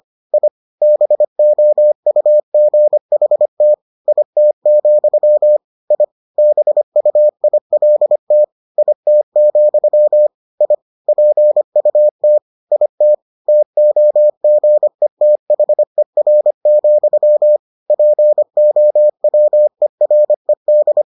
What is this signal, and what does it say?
Morse code, heard as a keyed tone, about 25 wpm, 600 Hz. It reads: I BOUGHT IT, I BUILT IT, I PUT IT TOGETHER, POWERED